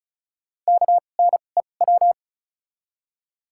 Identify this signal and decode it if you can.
Morse code, heard as a keyed tone, about 36 wpm, spaced as for 25 wpm, 700 Hz. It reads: KNEW
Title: 2x Character Spacing - Top 500 Words - 35wpm